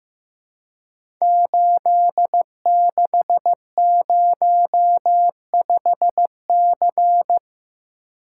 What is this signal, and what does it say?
Morse code, heard as a keyed tone, about 15 wpm, 700 Hz. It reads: 8605C
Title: Code Groups - 5 Letter:Numbers - 15wpm